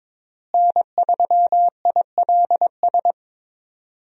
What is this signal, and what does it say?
Morse code, heard as a keyed tone, about 21 wpm, 700 Hz. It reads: N3ILS
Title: USA Call Signs - 5 Characters - Repeated 2x - 22wpm